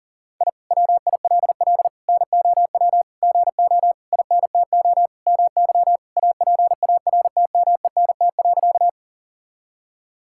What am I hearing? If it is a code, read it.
Morse code, heard as a keyed tone, about 40 wpm, 700 Hz. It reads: I WILL NOW GO INTO MY APARTMENT.
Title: Sentences from Top 1000 Words - Review Part 6 - 40wpm